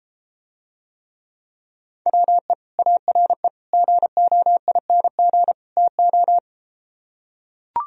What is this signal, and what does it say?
Morse code, heard as a keyed tone, about 33 wpm, 700 Hz. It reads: WE ARE GOING TO